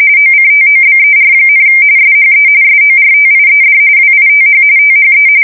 RTTY, about 45.45 baud. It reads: OCTOBER, SHE SORTIED WITH THE